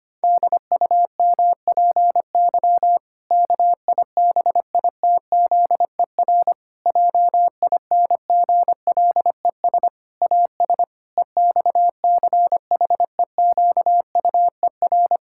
Morse code, heard as a keyed tone, about 25 wpm, 700 Hz. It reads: DUMPY KIBITZER JINGLES AS EXCHEQUER
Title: Pangram Challenge v2 - 25wpm